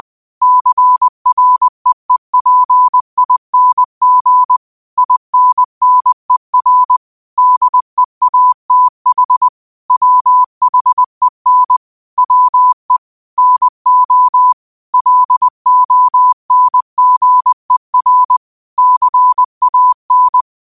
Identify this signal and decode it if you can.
Morse code, heard as a keyed tone, about 20 wpm, 1000 Hz. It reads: CREEPING INNER DEATH WHEN WE NO LONGER CAN
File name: qotdfiles / 20wpm